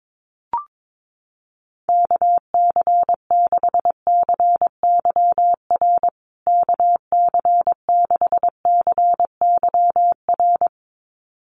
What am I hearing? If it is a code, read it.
Morse code, heard as a keyed tone, about 22 wpm, 700 Hz. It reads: KC6CYR KC6CYR
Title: USA Call Signs Repeated 2x 22wpm